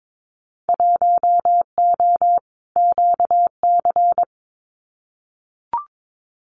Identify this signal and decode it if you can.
Morse code, heard as a keyed tone, about 22 wpm, 700 Hz. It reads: 1O QC
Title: Winter Field Day Exchanges 22wpm